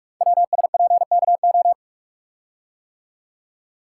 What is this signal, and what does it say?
Morse code, heard as a keyed tone, about 44 wpm, 700 Hz. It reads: WSPKO
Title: Code Groups - 5 Letters - 45wpm